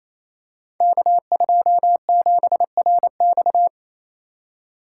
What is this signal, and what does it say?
Morse code, heard as a keyed tone, about 29 wpm, 700 Hz. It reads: K27RX
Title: Code Groups - 5 Letter:Numbers - 28wpm